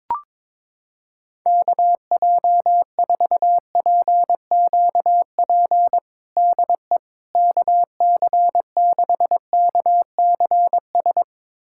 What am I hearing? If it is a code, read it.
Morse code, heard as a keyed tone, about 22 wpm, 700 Hz. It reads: KJ4PQP DE KC6KCS
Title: USA Callsign DE Callsign - 6 Characters - 22wpm